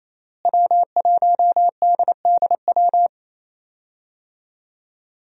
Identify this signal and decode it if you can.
Morse code, heard as a keyed tone, about 28 wpm, 700 Hz. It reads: W1DDW
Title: USA Call Signs - 5 Characters - Repeated 2x - 28wpm